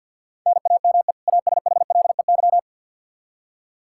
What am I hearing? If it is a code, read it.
Morse code, heard as a keyed tone, about 52 wpm, 700 Hz. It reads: NAME ASHLEY
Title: Practice QSO - Exchange 1 - Beginning - 50wpm